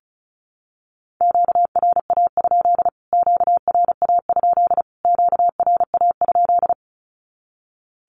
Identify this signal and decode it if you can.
Morse code, heard as a keyed tone, about 34 wpm, 700 Hz. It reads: QRA? QRA? QRA?
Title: Q-Codes Repeated 3x 35wpm